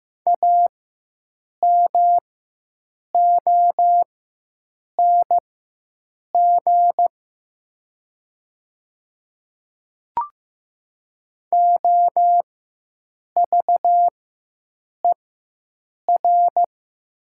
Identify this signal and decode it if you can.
Morse code, heard as a keyed tone, about 15 wpm, spaced as for 7 wpm, 700 Hz. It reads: AMONG OVER
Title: 4x Character Spacing - Top 500 Words - 15wpm